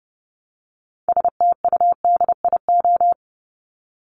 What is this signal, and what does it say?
Morse code, heard as a keyed tone, about 30 wpm, 700 Hz. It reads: STUDIO
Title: Top 1001-1250 Words 30wpm